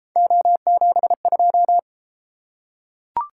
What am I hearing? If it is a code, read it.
Morse code, heard as a keyed tone, about 29 wpm, 700 Hz. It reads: O72